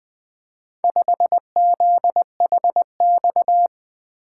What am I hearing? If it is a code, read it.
Morse code, heard as a keyed tone, about 24 wpm, 700 Hz. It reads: 5ZHX
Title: Speed-Racing - Four Letter-Number Combinations - 20wpm